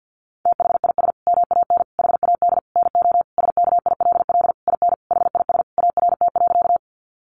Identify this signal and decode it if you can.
Morse code, heard as a keyed tone, about 50 wpm, 700 Hz. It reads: THIS MAN HAD NO SKILL IN HIS ART.
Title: Sentences from Top 1200 Words 50wpm